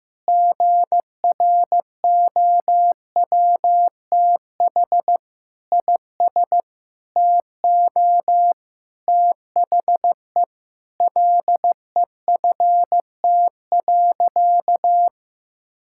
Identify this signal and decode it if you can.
Morse code, heard as a keyed tone, about 15 wpm, 700 Hz. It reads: GROWTH IS TO THE LEFT.
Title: Sentences from Top 1000 Words - Review Part 5 - 15wpm